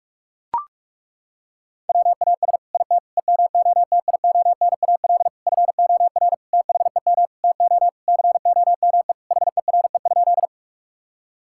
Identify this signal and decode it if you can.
Morse code, heard as a keyed tone, about 42 wpm, 700 Hz. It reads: WAS IT EMOTIONAL FOR THEM TO COME HERE?
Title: Sentences from Top 1000 Words - Review Part 5 - 45wpm